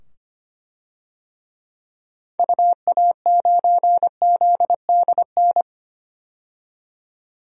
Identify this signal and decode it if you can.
Morse code, heard as a keyed tone, about 25 wpm, 700 Hz. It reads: UA9ZDN
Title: DXCC-100 Call Signs - 6 Characters - 25wpm